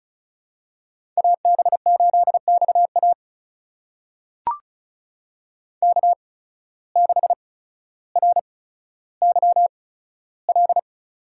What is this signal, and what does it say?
Morse code, heard as a keyed tone, about 36 wpm, spaced as for 15 wpm, 700 Hz. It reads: AB8XA K6RYL
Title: USA Call Signs - 5 Characters - 8x Character Spacing - 35wpm